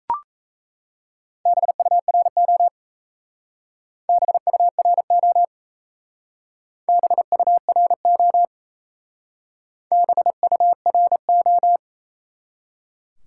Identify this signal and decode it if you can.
Morse code, heard as a keyed tone, about 35 wpm, 700 Hz. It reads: BURO BURO BURO BURO
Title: Speed-Racing Top 180 QSO Elements 28wpm